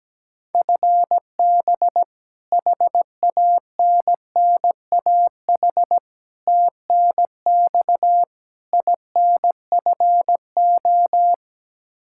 Morse code, heard as a keyed tone, about 17 wpm, 700 Hz. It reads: FB HANNAH TNX INFO
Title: Practice QSO - Exchange 3 - More Info - 17wpm